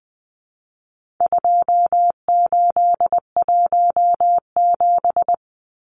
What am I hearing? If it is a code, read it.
Morse code, heard as a keyed tone, about 20 wpm, 700 Hz. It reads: 2817
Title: Numbers - 4-Digit 20wpm